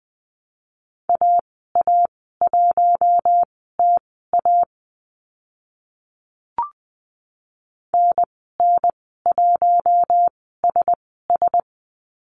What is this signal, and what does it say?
Morse code, heard as a keyed tone, about 20 wpm, spaced as for 14 wpm, 700 Hz. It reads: AA1TA NN1SS
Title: 042 Lesson 8g - Number 1 - Call Signs 20wpm